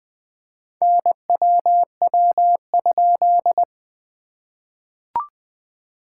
Morse code, heard as a keyed tone, about 20 wpm, 700 Hz. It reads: NWW?
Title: SOTA QSO Practice 20wpm